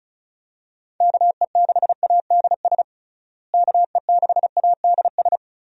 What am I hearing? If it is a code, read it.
Morse code, heard as a keyed tone, about 36 wpm, 700 Hz. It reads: KE6ADS KE6ADS
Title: USA Call Signs - 6 Characters - Repeated 2x - 35wpm